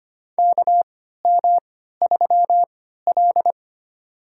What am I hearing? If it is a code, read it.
Morse code, heard as a keyed tone, about 25 wpm, spaced as for 14 wpm, 700 Hz. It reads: KM3L
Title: USA Call Signs - 4 Characters - 3x Character Spacing - 25wpm